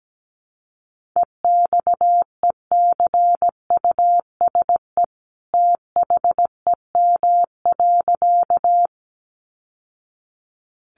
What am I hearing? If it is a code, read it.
Morse code, heard as a keyed tone, about 17 wpm, 700 Hz. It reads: EXECUSE THEM.
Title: Sentences from Top 700 Words 17wpm